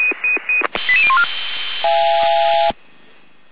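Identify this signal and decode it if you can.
Five-tone selective call (ZVEI), heard as quick runs of five tones, tones of about 70 ms. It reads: 88014
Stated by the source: ZVEI five-tone selective call received on an SDR.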